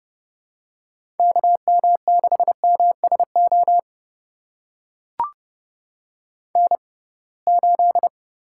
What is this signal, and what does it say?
Morse code, heard as a keyed tone, about 30 wpm, spaced as for 23 wpm, 700 Hz. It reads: KM6MSO N8
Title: USA Call Signs - 6x Character Spacing - 30wpm